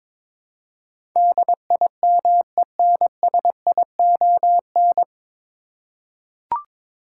Morse code, heard as a keyed tone, about 22 wpm, 700 Hz. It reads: DIMENSION
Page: Morse Code Ninja practice 22wpm